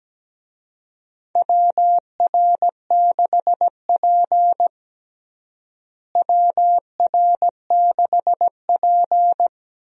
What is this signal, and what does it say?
Morse code, heard as a keyed tone, about 17 wpm, 700 Hz. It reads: WR6P WR6P
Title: USA Call Signs - 4 Characters - Repeated 2x - 17wpm